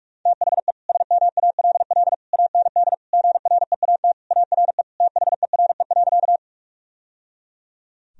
Morse code, heard as a keyed tone, about 46 wpm, 700 Hz. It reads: THE SMALL AND GREAT ARE THERE.
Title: Sentences from Top 1000 Words - Review Part 1 - 45wpm